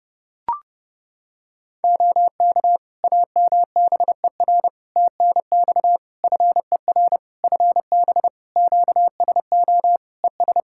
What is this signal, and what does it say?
Morse code, heard as a keyed tone, about 30 wpm, 700 Hz. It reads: OK AMBER TNX FER FB QSO ES
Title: Practice QSO - Exchange 4 - Ending - 30wpm